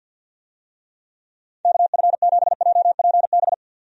Morse code, heard as a keyed tone, about 51 wpm, 700 Hz. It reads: KF7JPB
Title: USA Call Signs - 6 Characters - 50wpm